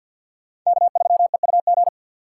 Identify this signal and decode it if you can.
Morse code, heard as a keyed tone, about 50 wpm, 700 Hz. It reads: K3EUG
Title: USA Call Signs - 5 Characters - 2x Character Spacing - 50wpm